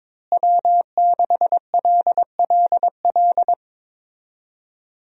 Morse code, heard as a keyed tone, about 21 wpm, 700 Hz. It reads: W6LLL
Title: USA Call Signs - 5 Characters - Repeated 3x - 22wpm